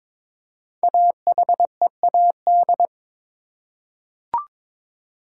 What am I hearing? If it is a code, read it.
Morse code, heard as a keyed tone, about 22 wpm, 700 Hz. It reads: AHEAD
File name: 5 Letter Words - First 3 Letters Given - 22wpm